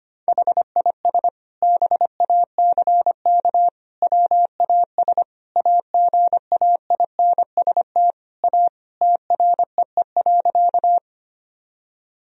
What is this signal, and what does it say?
Morse code, heard as a keyed tone, about 25 wpm, 700 Hz. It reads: HIS BACK WAS AGAINST A TREE.
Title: Sentences from Top 1000 Words - Review Part 1 - 25wpm